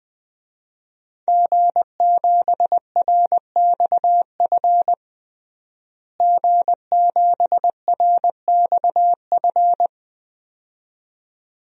Morse code, heard as a keyed tone, about 21 wpm, 700 Hz. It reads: G7RXF G7RXF
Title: DXCC-100 Call Signs - 5 Characters - Repeated 2x - 20wpm